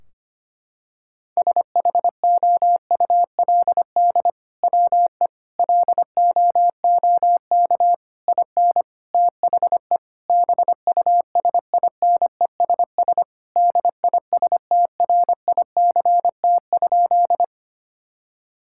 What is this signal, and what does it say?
Morse code, heard as a keyed tone, about 25 wpm, 700 Hz. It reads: SHOULD WE LOOK IN THE BUSINESS DISTRICT?
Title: Sentences from Top 1600 Words 25wpm